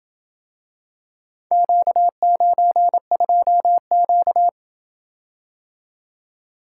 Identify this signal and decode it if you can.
Morse code, heard as a keyed tone, about 27 wpm, 700 Hz. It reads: Q92Q